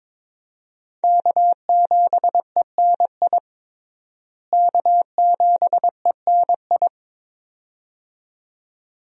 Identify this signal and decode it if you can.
Morse code, heard as a keyed tone, about 22 wpm, 700 Hz. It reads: K7ENI K7ENI
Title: USA Call Signs - 5 Characters - Repeated 2x - 22wpm